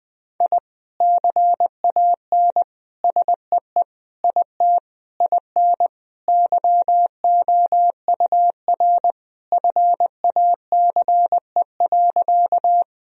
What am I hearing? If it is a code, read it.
Morse code, heard as a keyed tone, about 19 wpm, 700 Hz. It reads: I CAN SEE IT IN YOUR FACE.
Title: Sentences from Top 1000 Words - Review Part 2 - 20wpm